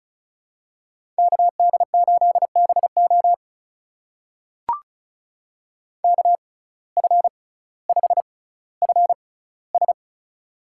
Morse code, heard as a keyed tone, about 36 wpm, spaced as for 18 wpm, 700 Hz. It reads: KD8BO KF5FS
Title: USA Call Signs - 5 Characters - 6x Character Spacing - 35wpm